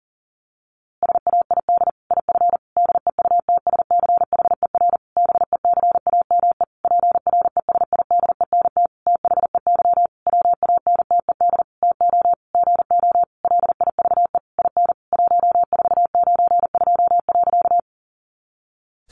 Morse code, heard as a keyed tone, about 40 wpm, 700 Hz. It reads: SAID IF DEUTSCHER BECAME PRESIDENT THEY WANTED TO GO LIVE IN 1492.
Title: A Sound Of Thunder 40wpm